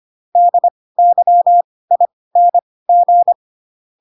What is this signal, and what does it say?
Morse code, heard as a keyed tone, about 25 wpm, spaced as for 18 wpm, 700 Hz. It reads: DYING